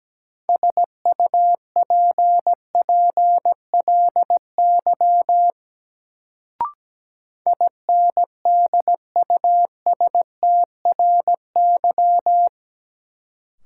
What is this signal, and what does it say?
Morse code, heard as a keyed tone, about 17 wpm, 700 Hz. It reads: SUPPLY INDUSTRY
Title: Sentences from Top 1000 Words - Review Part 4 - New Word Drill - 17wpm